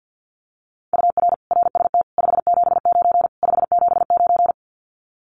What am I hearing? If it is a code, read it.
Morse code, heard as a keyed tone, about 48 wpm, 700 Hz. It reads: UR RST 579 579
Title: Common QSO Phrases Encore 50wpm